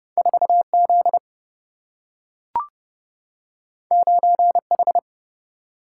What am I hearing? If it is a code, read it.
Morse code, heard as a keyed tone, about 30 wpm, 700 Hz. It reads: 4Z 9H
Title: Single-Letter Single-Number 30wpm